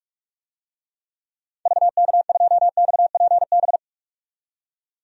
Morse code, heard as a keyed tone, about 44 wpm, 690 Hz. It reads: VK2XPB
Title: DXCC-100 Call Signs - 6 Characters - 45wpm